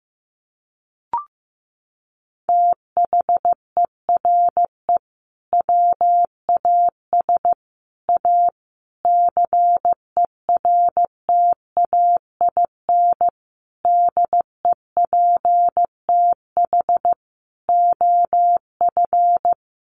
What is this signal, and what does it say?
Morse code, heard as a keyed tone, about 15 wpm, 700 Hz. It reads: THERE WAS A CERTAIN DEPTH OF
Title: Sentences from Top 1200 Words 15wpm